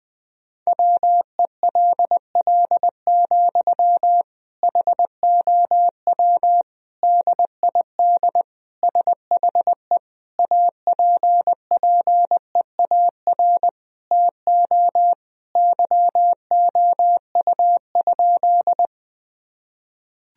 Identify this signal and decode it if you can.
Morse code, heard as a keyed tone, about 20 wpm, 700 Hz. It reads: WELL, HOW DID SHE APPEAR TO YOU?
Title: Sentences from Top 1000 Words - Review Part 2 - 20wpm